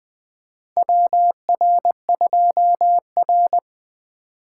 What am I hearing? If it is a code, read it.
Morse code, heard as a keyed tone, about 20 wpm, 700 Hz. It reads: WR2R
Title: USA Call Signs - 4 Characters - 4x Character Spacing - 20wpm